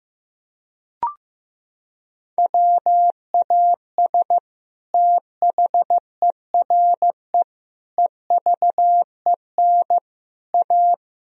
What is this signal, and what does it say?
Morse code, heard as a keyed tone, about 15 wpm, 700 Hz. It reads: WAS THERE EVEN A
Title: Sentences from Top 1000 Words - Review Part 4 - 15wpm